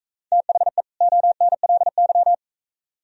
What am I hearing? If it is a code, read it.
Morse code, heard as a keyed tone, about 42 wpm, 700 Hz. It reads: THE ONLY